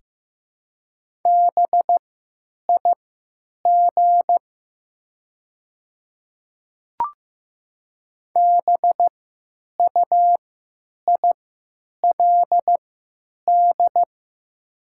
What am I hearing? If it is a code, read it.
Morse code, heard as a keyed tone, about 15 wpm, spaced as for 9 wpm, 700 Hz. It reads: BIG BUILD
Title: 3x Character Spacing - Top 500 Words - 15wpm